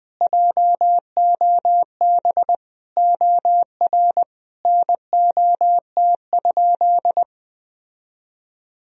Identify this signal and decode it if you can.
Morse code, heard as a keyed tone, about 20 wpm, 700 Hz. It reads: JOB OR NOT?
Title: Sentences from Top 1000 Words - Review Part 3 - 20wpm